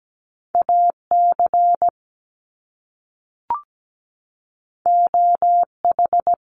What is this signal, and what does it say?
Morse code, heard as a keyed tone, about 17 wpm, 700 Hz. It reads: AC OH
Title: Top Two Letter Combinations in Words 17wpm